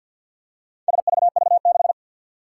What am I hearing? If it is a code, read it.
Morse code, heard as a keyed tone, about 47 wpm, 700 Hz. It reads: SVV6
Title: Four Challenging Letter-Number Combinations 50wpm